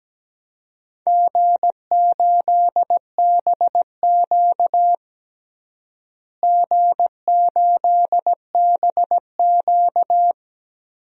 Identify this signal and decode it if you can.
Morse code, heard as a keyed tone, about 17 wpm, 700 Hz. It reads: G8BQ G8BQ
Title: DXCC-100 Call Signs - 4 Characters - Repeated 2x - 17wpm